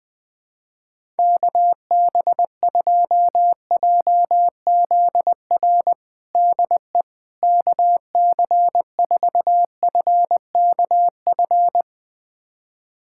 Morse code, heard as a keyed tone, about 20 wpm, 700 Hz. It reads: KB2JZR DE KC4FKF
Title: USA Callsign DE Callsign - 6 Characters - 20wpm